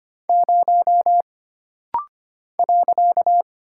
Morse code, heard as a keyed tone, about 25 wpm, 700 Hz. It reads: 0 .
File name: Single Character (Letter:Number:Punctuation:Prosign) - Rapid-Fire - 25wpm